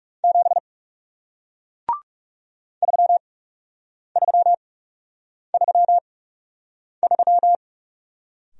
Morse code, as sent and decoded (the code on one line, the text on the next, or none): --... / ...-- ...-- ...-- ...--
7 3333